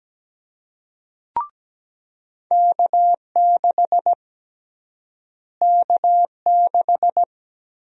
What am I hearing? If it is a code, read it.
Morse code, heard as a keyed tone, about 17 wpm, 700 Hz. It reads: K6 K6
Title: USA Call Sign Prefixes - Repeated 2x - 17wpm